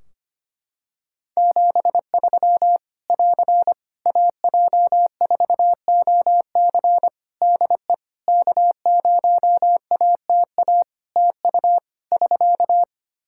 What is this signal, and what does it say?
Morse code, heard as a keyed tone, about 25 wpm, 700 Hz. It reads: 73 <AR> AJ4OC DE K0ATA TU <SK>
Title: Practice QSO - Exchange 4 - Ending - 25wpm